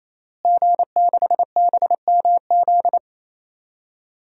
Morse code, heard as a keyed tone, about 28 wpm, 700 Hz. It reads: G6BMZ
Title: DXCC-100 Call Signs - 5 Characters - Repeated 3x - 28wpm